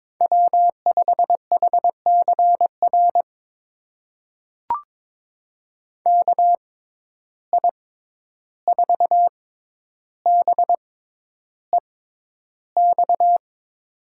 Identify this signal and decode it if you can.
Morse code, heard as a keyed tone, about 22 wpm, spaced as for 11 wpm, 700 Hz. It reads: W5HCR KI4BEX
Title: USA Call Signs - 6x Character Spacing - 22wpm